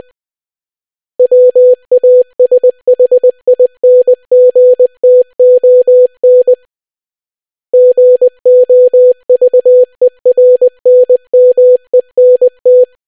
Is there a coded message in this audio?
Yes: Morse code from 1.2 to 12.8 s, heard as a keyed tone, about 20 wpm, 500 Hz. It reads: WASHINGTON GOVERNMENT